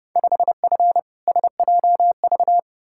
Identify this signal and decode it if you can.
Morse code, heard as a keyed tone, about 32 wpm, 700 Hz. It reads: 5F SJV
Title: Field Day Exchanges Encore - 30wpm